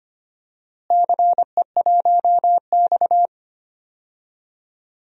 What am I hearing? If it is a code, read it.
Morse code, heard as a keyed tone, about 25 wpm, 700 Hz. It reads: CE1X
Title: DXCC-100 Call Signs - 4 Characters - Repeated 2x - 25wpm